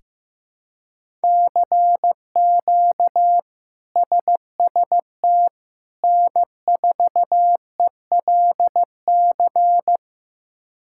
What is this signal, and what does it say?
Morse code, heard as a keyed tone, about 15 wpm, 700 Hz. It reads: CQ SST N4ELC
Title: K1USN SST QSO Practice - 15wpm